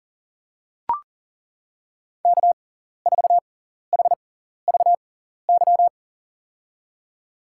Morse code, heard as a keyed tone, about 40 wpm, spaced as for 14 wpm, 700 Hz. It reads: K4HVY
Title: USA Call Signs - 6x Character Spacing - 40wpm